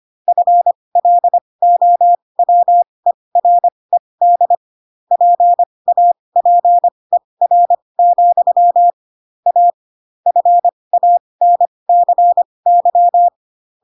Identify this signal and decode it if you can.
Morse code, heard as a keyed tone, about 25 wpm, spaced as for 20 wpm, 700 Hz. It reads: FLOWERED PAPER, A FANCY
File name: Pinocchio / Pinocchio ch8 15 25 ik1hge